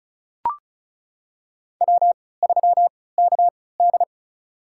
Morse code, heard as a keyed tone, about 36 wpm, spaced as for 20 wpm, 700 Hz. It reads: W3KD
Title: USA Call Signs - 4 Characters - 3x Character Spacing - 35wpm